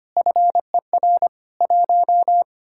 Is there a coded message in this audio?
Yes: Morse code from 0.2 to 2.4 s, heard as a keyed tone, about 33 wpm, 700 Hz. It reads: FER 1